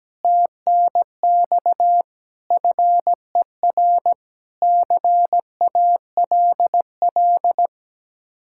Morse code, heard as a keyed tone, about 17 wpm, 700 Hz. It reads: TNX FER CALL
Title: Speed-Racing Common QSO Phrases - 17wpm